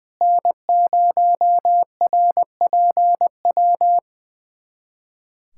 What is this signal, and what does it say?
Morse code, heard as a keyed tone, about 21 wpm, 700 Hz. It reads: N0RPW
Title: USA Callsign DE Callsign - 5 Characters - 20wpm